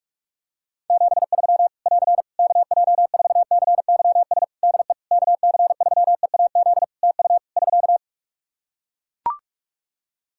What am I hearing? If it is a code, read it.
Morse code, heard as a keyed tone, about 44 wpm, 700 Hz. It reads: 73 <AR> KJ4CYS DE KC3EAZ TU <SK>
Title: Practice QSO - Exchange 4 - Ending - 45wpm